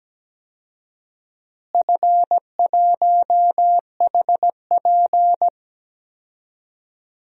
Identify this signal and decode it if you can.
Morse code, heard as a keyed tone, about 17 wpm, 700 Hz. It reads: F1HP
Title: DXCC-100 Call Signs - 4 Characters - Repeated 3x - 17wpm